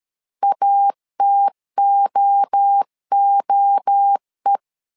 Morse code, heard as a keyed tone, about 13 wpm, 800 Hz. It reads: ATOOE
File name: LEZIONI / 3 MWRO / 3 MWRO 12wpm 1730